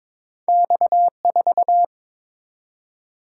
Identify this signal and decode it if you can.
Morse code, heard as a keyed tone, about 30 wpm, spaced as for 26 wpm, 700 Hz. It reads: X4
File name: Speed-Racing - Two Letter-Number Combinations - 22wpm